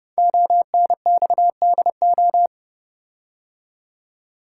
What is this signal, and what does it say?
Morse code, heard as a keyed tone, about 30 wpm, 700 Hz. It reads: ONXDO
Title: Code Groups - 5 Letters - 30wpm